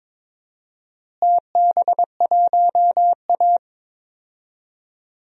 Morse code, heard as a keyed tone, about 21 wpm, 700 Hz. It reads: TB1A
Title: DXCC-100 Call Signs - 4 Characters - Repeated 3x - 22wpm